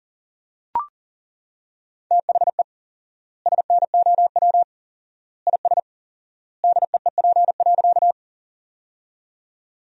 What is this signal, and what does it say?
Morse code, heard as a keyed tone, about 40 wpm, 700 Hz. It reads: THE SNOW IS DEEP.
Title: 4x Word-Spacing - Sentences from Top 500 Words - 40wpm